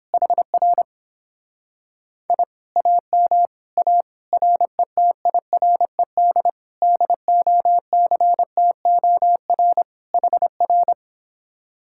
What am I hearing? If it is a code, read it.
Morse code, heard as a keyed tone, about 29 wpm, 700 Hz. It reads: HR I AM A RETIRED DOCTOR HR